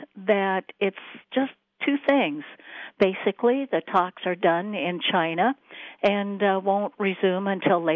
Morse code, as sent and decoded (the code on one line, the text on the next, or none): none